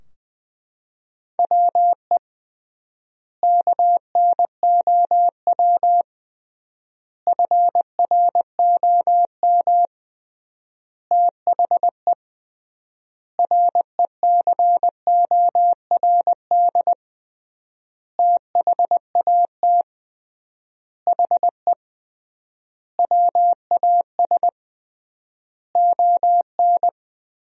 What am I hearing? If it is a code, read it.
Morse code, heard as a keyed tone, about 20 wpm, 700 Hz. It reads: WE KNOW FROM THE RECORD THAT HE WAS ON
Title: 3x Word-Spacing - Sentences from Top 500 Words - 20wpm